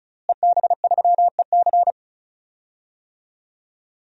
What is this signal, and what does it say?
Morse code, heard as a keyed tone, about 34 wpm, 700 Hz. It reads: EB3EC
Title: DXCC-100 Call Signs - 5 Characters - Repeated 2x - 35wpm